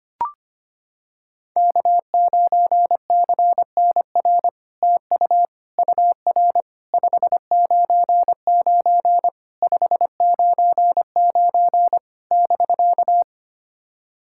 Morse code, heard as a keyed tone, about 25 wpm, 700 Hz. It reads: K9CNR TU UR 599 599 <BK>
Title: SOTA QSO Practice 25wpm